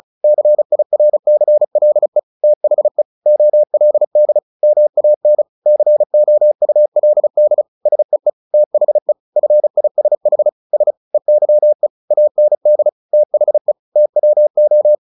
Morse code, heard as a keyed tone, about 34 wpm, 600 Hz. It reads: CIRCLE THE OLD MAN COULD SEE THE FISH S EYE AND THE TWO
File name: the old man and the sea 2 35wpm